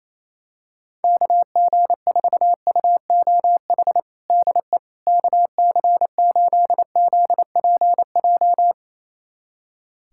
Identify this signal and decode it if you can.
Morse code, heard as a keyed tone, about 28 wpm, 700 Hz. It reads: KG4UOH DE KC8ZPJ
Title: USA Callsign DE Callsign - 6 Characters - 28wpm